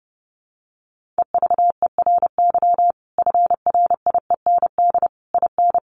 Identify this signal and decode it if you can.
Morse code, heard as a keyed tone, about 30 wpm, 700 Hz. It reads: EVERY FRIEND IN
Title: Sentences from Top 1400 Words 30wpm